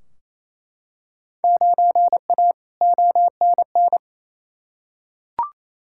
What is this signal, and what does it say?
Morse code, heard as a keyed tone, about 28 wpm, 700 Hz. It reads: 9A ONN
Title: Field Day Exchanges Encore - 28wpm